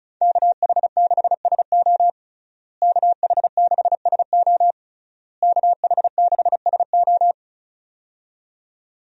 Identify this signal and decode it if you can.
Morse code, heard as a keyed tone, about 36 wpm, 700 Hz. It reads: KH6SO KH6SO KH6SO
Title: USA Call Signs - 5 Characters - Repeated 3x - 35wpm